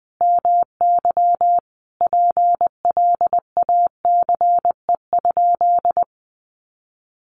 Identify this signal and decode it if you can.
Morse code, heard as a keyed tone, about 20 wpm, 700 Hz. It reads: MY PLACE?
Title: Sentences from Top 200 Words 20wpm